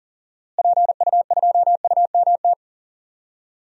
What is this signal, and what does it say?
Morse code, heard as a keyed tone, about 39 wpm, 700 Hz. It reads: PU2UMT
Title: DXCC-100 Call Signs - 6 Characters - Repeated 3x - 40wpm